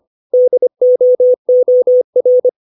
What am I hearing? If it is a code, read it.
Morse code, heard as a keyed tone, about 25 wpm, 500 Hz. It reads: DOOR
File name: 632 words / 632 Words 25WPM